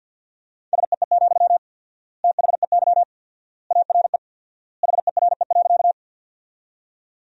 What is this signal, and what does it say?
Morse code, heard as a keyed tone, about 52 wpm, 700 Hz. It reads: SEE, THEY ARE HERE.
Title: 4x Word-Spacing - Sentences from Top 200 Words - 50wpm